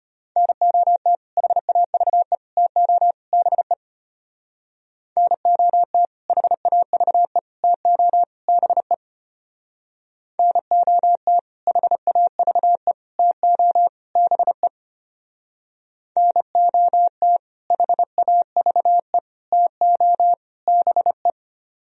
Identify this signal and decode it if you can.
Morse code, heard as a keyed tone, about 32 wpm, 700 Hz. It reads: NOT HAVE TO BE NOT HAVE TO BE NOT HAVE TO BE NOT HAVE TO BE